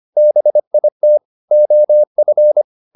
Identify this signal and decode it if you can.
Morse code, heard as a keyed tone, about 25 wpm, 600 Hz. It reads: BIT OF